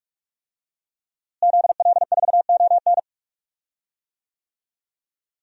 Morse code, heard as a keyed tone, about 41 wpm, 700 Hz. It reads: ZL4ON
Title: DXCC-100 Call Signs - 5 Characters - Repeated 3x - 45wpm